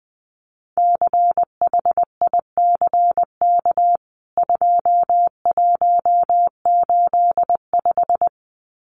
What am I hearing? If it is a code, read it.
Morse code, heard as a keyed tone, about 20 wpm, 700 Hz. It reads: CHICK 2185
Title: CWT Exchange 20wpm